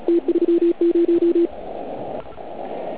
Morse code, heard as a keyed tone, about 36 wpm, 340 Hz, with radio noise. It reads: T30
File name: Archiv / CUC129 / CUC129 pileup